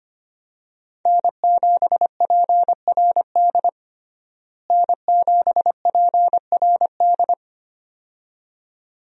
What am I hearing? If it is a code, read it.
Morse code, heard as a keyed tone, about 25 wpm, 700 Hz. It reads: N7PRD N7PRD
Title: USA Call Signs - 5 Characters - Repeated 2x - 25wpm